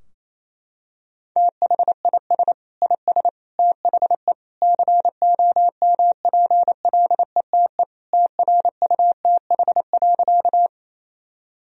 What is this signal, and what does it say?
Morse code, heard as a keyed tone, about 28 wpm, 700 Hz. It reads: THIS IS THE COMPLETE TRUTH.
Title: Sentences from Top 1000 Words - Review Part 2 - 28wpm